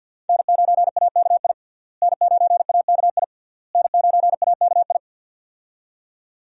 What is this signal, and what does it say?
Morse code, heard as a keyed tone, about 51 wpm, 700 Hz. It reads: N9AKI N9AKI N9AKI
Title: USA Call Signs - 5 Characters - Repeated 3x - 50wpm